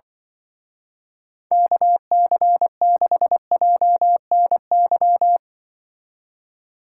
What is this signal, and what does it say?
Morse code, heard as a keyed tone, about 24 wpm, 700 Hz. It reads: KC6JNY
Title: Speed Racing - USA 2x3 Call Signs 20wpm